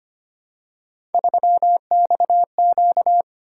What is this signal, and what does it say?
Morse code, heard as a keyed tone, about 25 wpm, 700 Hz. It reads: 3XQ
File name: Three Letter-Number Combinations - 25wpm